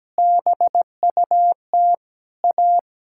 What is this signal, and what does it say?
Morse code, heard as a keyed tone, about 17 wpm, 700 Hz. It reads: BUT A
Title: Sentences from Top 1000 Words - Review Part 6 - 17wpm